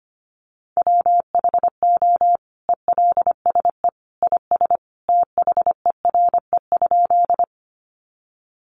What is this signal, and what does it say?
Morse code, heard as a keyed tone, about 25 wpm, 700 Hz. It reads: WHO ELSE IS THERE?
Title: Sentences from Top 800 Words 25wpm